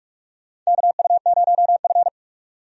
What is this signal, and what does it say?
Morse code, heard as a keyed tone, about 42 wpm, 700 Hz. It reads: KU0F
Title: USA Call Signs - 4 Characters - 45wpm